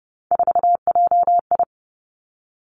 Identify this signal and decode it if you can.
Morse code, heard as a keyed tone, about 30 wpm, 700 Hz. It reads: 4JI
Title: Three Challenging Letter-Number Combinations 30wpm